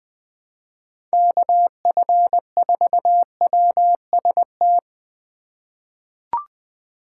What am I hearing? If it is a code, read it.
Morse code, heard as a keyed tone, about 21 wpm, 700 Hz. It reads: KF4WST
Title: USA Call Signs - 6 Characters - 2x Character Spacing - 20wpm